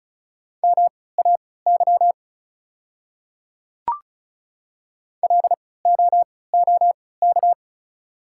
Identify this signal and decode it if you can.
Morse code, heard as a keyed tone, about 36 wpm, spaced as for 20 wpm, 690 Hz. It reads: MAY LOOK
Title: 3x Character Spacing - Top 100 Words - 35wpm